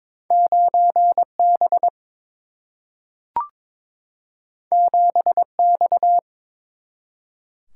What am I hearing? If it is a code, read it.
Morse code, heard as a keyed tone, about 22 wpm, 700 Hz. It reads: 9 B 7 X
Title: Single-Letter Single-Number 22wpm